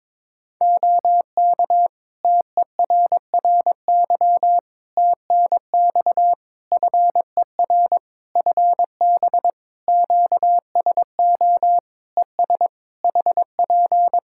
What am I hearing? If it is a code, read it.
Morse code, heard as a keyed tone, about 22 wpm, 700 Hz. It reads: OK TERRY TNX FER FB QSO ES HP
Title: Practice QSO - Exchange 4 - Ending - 22wpm